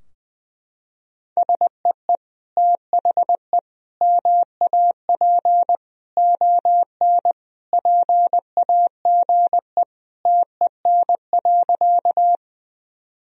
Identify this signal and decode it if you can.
Morse code, heard as a keyed tone, about 20 wpm, 700 Hz. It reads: SEE THE MAP ON PAGE TEN.
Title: Sentences from Top 1000 Words - Review Part 2 - 20wpm